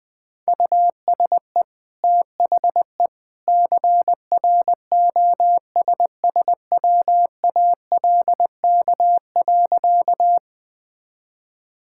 Morse code, heard as a keyed tone, about 20 wpm, 700 Hz. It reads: USE THE CROSSWALK.
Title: Easy Sentences with Hints - 20wpm